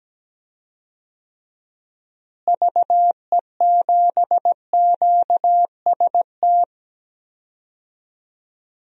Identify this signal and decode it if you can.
Morse code, heard as a keyed tone, about 17 wpm, 700 Hz. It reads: VE7QST
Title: DXCC-100 Call Signs - 6 Characters - Repeated 3x - 17wpm